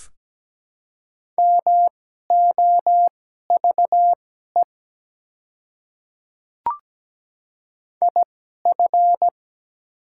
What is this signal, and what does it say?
Morse code, heard as a keyed tone, about 17 wpm, spaced as for 12 wpm, 690 Hz. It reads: MOVE IF
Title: 2x Character Spacing - Top 300 Words - 17wpm